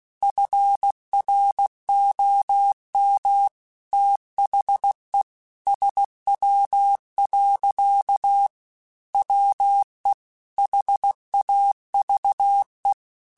Morse code, heard as a keyed tone, about 16 wpm, 800 Hz. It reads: FROM THE SW. WE HAVE
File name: Weather Sets / Weather Sets-mp3s / Weather 5 5 @15